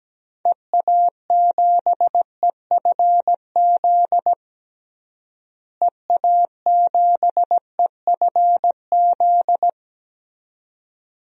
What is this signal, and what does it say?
Morse code, heard as a keyed tone, about 17 wpm, 700 Hz. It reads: EA7EFZ EA7EFZ
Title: DXCC-100 Call Signs - 6 Characters - Repeated 2x - 17wpm